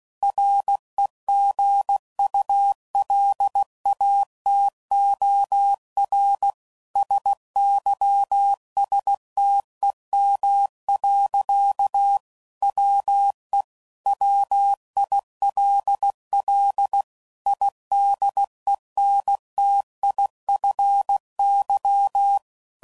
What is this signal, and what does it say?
Morse code, heard as a keyed tone, about 16 wpm, 800 Hz. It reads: REGULATOR SYSTEM. WE WILL IDENTIFY
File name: CW Code Groups @15WPM / Technical Subjects / PwrSupply